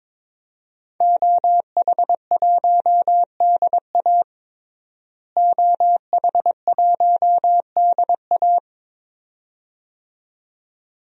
Morse code, heard as a keyed tone, about 22 wpm, 700 Hz. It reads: OH1DA OH1DA
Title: DXCC-100 Call Signs - 5 Characters - Repeated 2x - 22wpm